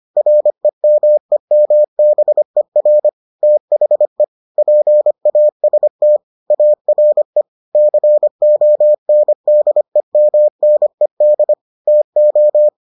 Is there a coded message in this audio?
Yes: Morse code from 0.2 to 12.7 s, heard as a keyed tone, about 25 wpm, 600 Hz. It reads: REMEMBER THE PAST ARE CONDEMNED TO